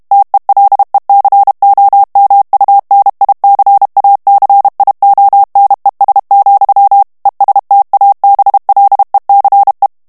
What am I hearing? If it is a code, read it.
Morse code, heard as a keyed tone, about 32 wpm, 800 Hz. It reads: TELECOMUNICACIONES, ESTABLECE
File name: morse mp3 / ESPAGNOL TEXTE EN CLAIR 2 VITESSE 30 MOTS MINUTES